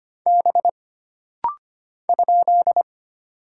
Morse code, heard as a keyed tone, about 25 wpm, 700 Hz. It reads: B ?
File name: Single Character (Letter:Number:Punctuation:Prosign) - Rapid-Fire - 25wpm